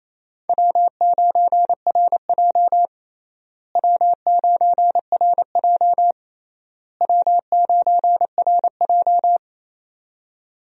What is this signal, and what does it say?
Morse code, heard as a keyed tone, about 28 wpm, 700 Hz. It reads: W9RJ W9RJ W9RJ
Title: USA Call Signs - 4 Characters - Repeated 3x - 28wpm